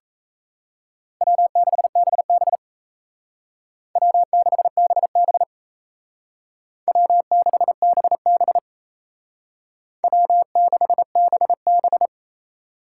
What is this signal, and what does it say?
Morse code, heard as a keyed tone, about 36 wpm, 700 Hz. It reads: W6BB W6BB W6BB W6BB